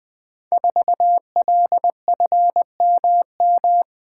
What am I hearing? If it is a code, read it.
Morse code, heard as a keyed tone, about 20 wpm, 700 Hz. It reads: 4LFMM
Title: Code Groups - 5 Letter:Numbers - 20wpm